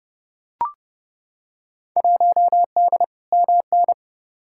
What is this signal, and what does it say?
Morse code, heard as a keyed tone, about 30 wpm, 700 Hz. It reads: 1D MN
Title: Field Day Exchanges - 30wpm